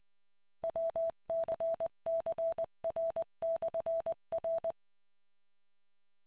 Morse code, heard as a keyed tone, about 20 wpm, 660 Hz. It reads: WCCR/R